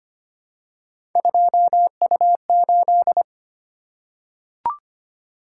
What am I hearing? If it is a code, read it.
Morse code, heard as a keyed tone, about 25 wpm, 700 Hz. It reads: 2U8
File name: Three Letter-Number Combinations - 25wpm